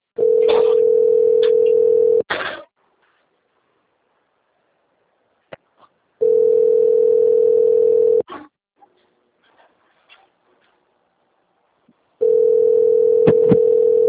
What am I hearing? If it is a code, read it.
Ringback tone (North America), 440 + 480 Hz.